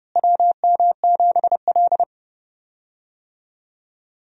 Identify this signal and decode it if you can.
Morse code, heard as a keyed tone, about 30 wpm, 700 Hz. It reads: WM7L
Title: USA Call Signs - 4 Characters - Repeated 3x - 30wpm